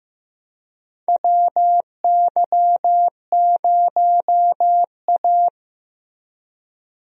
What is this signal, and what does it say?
Morse code, heard as a keyed tone, about 15 wpm, 700 Hz. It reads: WY0A
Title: USA Call Signs - 4 Characters - Repeated 2x - 15wpm